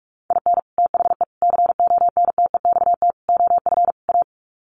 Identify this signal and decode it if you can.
Morse code, heard as a keyed tone, about 44 wpm, 700 Hz. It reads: IN THE CONTEXT OF A
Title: Sets of 5 Words Encore 45wpm